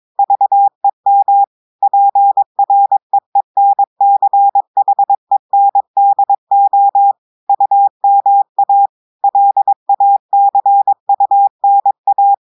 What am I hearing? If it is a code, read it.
Morse code, heard as a keyed tone, about 22 wpm, 800 Hz. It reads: VEM PREENCHENDO UMA LACUNA
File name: audios / portugais / portugais texte en clair 1 vitesse 22 mots minutes